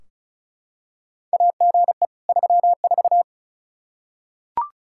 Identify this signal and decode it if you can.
Morse code, heard as a keyed tone, about 36 wpm, 700 Hz. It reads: AGE 34
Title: Practice QSO - Exchange 3 - More Info - 35wpm